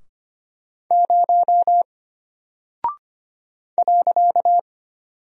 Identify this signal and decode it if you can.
Morse code, heard as a keyed tone, about 25 wpm, 700 Hz. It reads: 0 .
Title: Single Character (Letter:Number:Punctuation:Prosign) Alt 25wpm